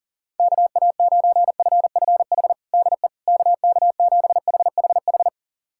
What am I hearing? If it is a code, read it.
Morse code, heard as a keyed tone, about 37 wpm, 700 Hz. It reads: KA9FFH DE KK7HHH
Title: USA Callsign DE Callsign - 6 Characters - 40wpm